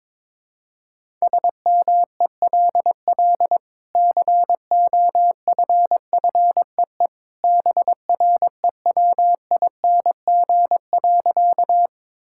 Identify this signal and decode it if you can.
Morse code, heard as a keyed tone, about 22 wpm, 700 Hz. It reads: SMELL COFFEE BREWING.
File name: practice / 22wpm / Easy Sentences with Hints - 22wpm